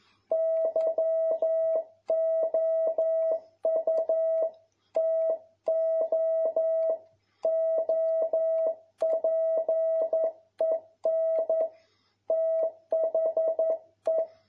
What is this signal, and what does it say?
Morse code, heard as a keyed tone, about 11 wpm, 650 Hz. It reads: YOU TO OPEN THE
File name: jh8cbh / 2021cw